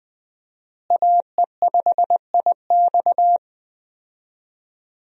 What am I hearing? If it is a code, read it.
Morse code, heard as a keyed tone, about 20 wpm, 700 Hz. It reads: AE5IX
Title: USA Call Signs - 5 Characters - Repeated 3x - 20wpm